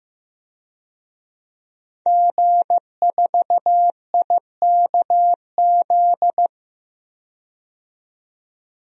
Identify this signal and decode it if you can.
Morse code, heard as a keyed tone, about 15 wpm, 700 Hz. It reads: G4IKZ
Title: DXCC-100 Call Signs - 5 Characters - 15wpm